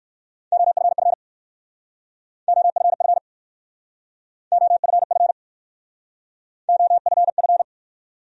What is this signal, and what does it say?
Morse code, heard as a keyed tone, about 60 wpm, 690 Hz. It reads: OFF OFF OFF OFF